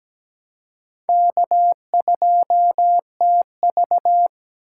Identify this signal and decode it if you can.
Morse code, heard as a keyed tone, about 17 wpm, 700 Hz. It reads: K2TV
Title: K1USN SST QSO Practice - 17wpm